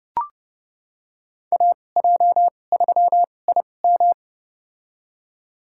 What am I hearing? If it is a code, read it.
Morse code, heard as a keyed tone, about 30 wpm, spaced as for 22 wpm, 700 Hz. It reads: AJ3IM
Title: USA Call Signs - 5 Characters - 2x Character Spacing - 30wpm